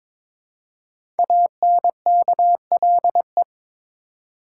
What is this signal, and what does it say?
Morse code, heard as a keyed tone, about 22 wpm, 700 Hz. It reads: ANKLE
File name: 5 Letter Words - First 3 Letters Given - 22wpm